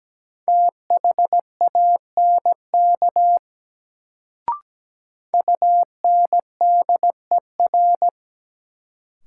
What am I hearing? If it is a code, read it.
Morse code, heard as a keyed tone, about 17 wpm, 700 Hz. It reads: THANK UNDER
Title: Sets of 2 Words - New Word Drill - 17wpm